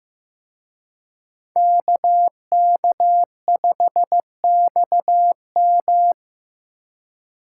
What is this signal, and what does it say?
Morse code, heard as a keyed tone, about 19 wpm, 700 Hz. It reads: KK5XM
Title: Speed Racing - USA 2x2 Call Signs 15wpm